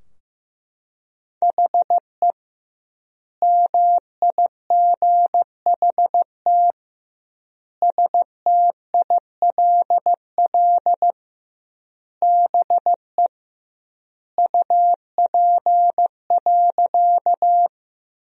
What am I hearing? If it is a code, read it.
Morse code, heard as a keyed tone, about 16 wpm, 700 Hz. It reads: HE MIGHT STILL BE UP.
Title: 2x Word-Spacing - Sentences from Top 300 Words - 15wpm